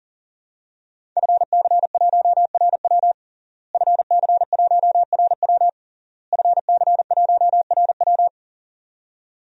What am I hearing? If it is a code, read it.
Morse code, heard as a keyed tone, about 40 wpm, 700 Hz. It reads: FC1RW FC1RW FC1RW
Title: DXCC-100 Call Signs - 5 Characters - Repeated 3x - 40wpm